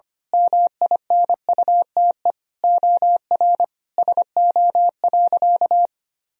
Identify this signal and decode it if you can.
Morse code, heard as a keyed tone, about 25 wpm, 700 Hz. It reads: MINUTE OR SO.
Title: Sentences from Top 1000 Words - Review Part 4 - 25wpm